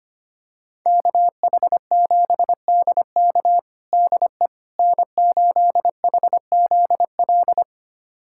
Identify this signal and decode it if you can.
Morse code, heard as a keyed tone, about 25 wpm, 700 Hz. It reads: KH7DK DE N8HZL
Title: USA Callsign DE Callsign - 5 Characters - 25wpm